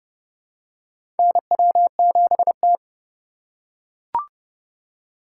Morse code, heard as a keyed tone, about 30 wpm, 700 Hz. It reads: NW7T
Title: USA Call Signs - 4 Characters - 8x Character Spacing - 30wpm